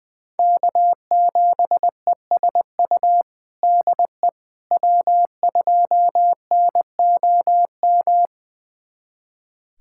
Morse code, heard as a keyed tone, about 20 wpm, 700 Hz. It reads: K7ESU DE W2NOM
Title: USA Callsign DE Callsign - 5 Characters - 20wpm